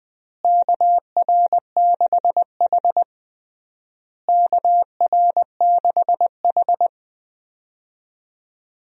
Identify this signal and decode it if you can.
Morse code, heard as a keyed tone, about 21 wpm, 700 Hz. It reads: KR6H KR6H
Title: USA Call Signs - 4 Characters - Repeated 3x - 20wpm